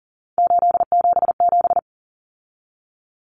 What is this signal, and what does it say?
Morse code, heard as a keyed tone, about 40 wpm, 700 Hz. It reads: 877
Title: Common Cut Numbers - 3-Digit 40wpm